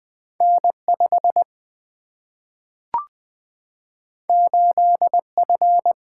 Morse code, heard as a keyed tone, about 20 wpm, 700 Hz. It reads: N 5 8 F
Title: Single-Letter Single-Number 20wpm